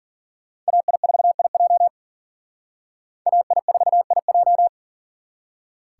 Morse code, heard as a keyed tone, about 51 wpm, 700 Hz. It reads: AI4IJ AI4IJ